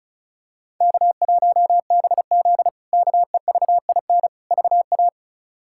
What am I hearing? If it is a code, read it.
Morse code, heard as a keyed tone, about 34 wpm, 700 Hz. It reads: K1BZ KEVIN VA
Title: K1USN SST QSO Practice - 35wpm